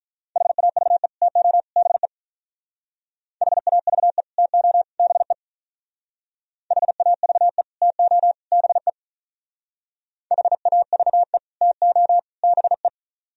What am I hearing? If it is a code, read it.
Morse code, heard as a keyed tone, about 43 wpm, 700 Hz. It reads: HAVE TO BE HAVE TO BE HAVE TO BE HAVE TO BE
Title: Speed-Racing - Sets of 3 Words 35wpm